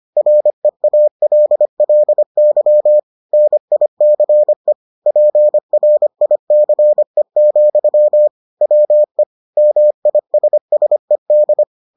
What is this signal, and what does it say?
Morse code, heard as a keyed tone, about 25 wpm, 600 Hz. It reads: REALLY NICE PRICE, WE MISSED